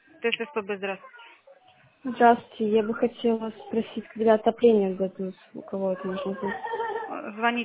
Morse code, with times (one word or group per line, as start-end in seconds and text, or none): none